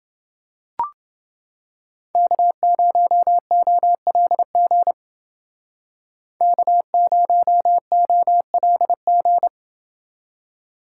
Morse code, heard as a keyed tone, about 28 wpm, 700 Hz. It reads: K0OLG K0OLG